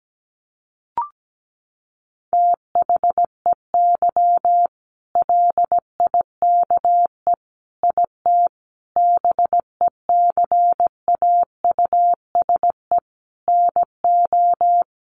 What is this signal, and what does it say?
Morse code, heard as a keyed tone, about 17 wpm, 700 Hz. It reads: THEY LIKE IT BECAUSE NO
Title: Sentences from Top 600 Words 17wpm